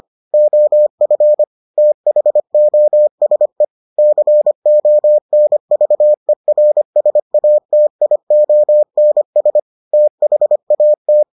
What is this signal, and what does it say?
Morse code, heard as a keyed tone, about 25 wpm, 600 Hz. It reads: OF THOSE CONVERSATIONS THAT